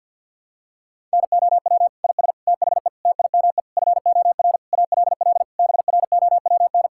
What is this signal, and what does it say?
Morse code, heard as a keyed tone, about 51 wpm, 700 Hz. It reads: NOW IS THE TIME FOR ALL BROWN
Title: Pangram Challenge v2 - 50wpm